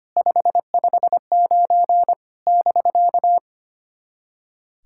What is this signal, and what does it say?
Morse code, heard as a keyed tone, about 25 wpm, 660 Hz. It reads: 559 <BK>
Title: SOTA QSO Practice 25wpm